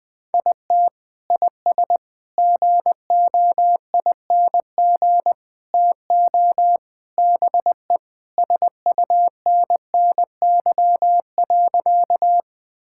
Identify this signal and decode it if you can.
Morse code, heard as a keyed tone, about 20 wpm, 700 Hz. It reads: IT IS GOING TO BE SUNNY.
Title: Sentences with 4-Word Phrases - 20wpm